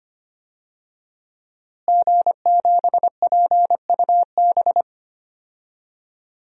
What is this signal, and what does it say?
Morse code, heard as a keyed tone, about 25 wpm, 700 Hz. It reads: G7PUB
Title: DXCC-100 Call Signs - 5 Characters - Repeated 3x - 25wpm